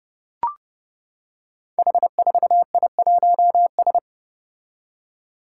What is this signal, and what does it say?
Morse code, heard as a keyed tone, about 30 wpm, 700 Hz. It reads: H4I1S
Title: Five Challenging Letter-Number Combinations 30wpm